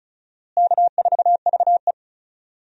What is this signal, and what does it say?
Morse code, heard as a keyed tone, about 36 wpm, 700 Hz. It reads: K4VE
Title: USA Call Signs - 4 Characters - Repeated 2x - 35wpm